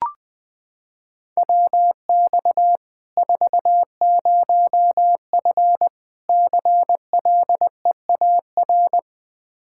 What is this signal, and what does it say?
Morse code, heard as a keyed tone, about 20 wpm, 700 Hz. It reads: WX 40F CLEAR
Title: Practice QSO - Exchange 2 - Info - 20wpm